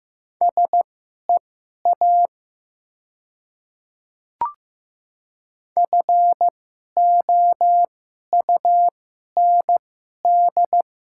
Morse code, heard as a keyed tone, about 15 wpm, spaced as for 11 wpm, 700 Hz. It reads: SEA FOUND
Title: 2x Character Spacing - Top 300 Words - 15wpm